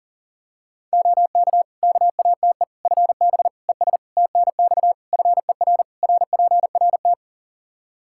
Morse code, heard as a keyed tone, about 40 wpm, 700 Hz. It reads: OK KATE FB ES TNX FER RPRT
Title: Practice QSO - Exchange 2 - Info - 40wpm